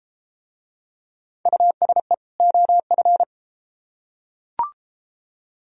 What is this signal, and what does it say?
Morse code, heard as a keyed tone, about 30 wpm, 700 Hz. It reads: USE OF